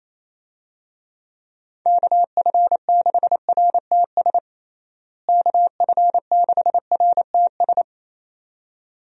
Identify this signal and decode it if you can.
Morse code, heard as a keyed tone, about 28 wpm, 700 Hz. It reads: KF6RTS KF6RTS
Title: USA Call Signs - 6 Characters - Repeated 2x - 28wpm